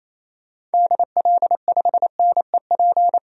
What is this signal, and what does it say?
Morse code, heard as a keyed tone, about 28 wpm, 700 Hz. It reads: DL5NEP
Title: DXCC-100 Call Signs - 6 Characters - 28wpm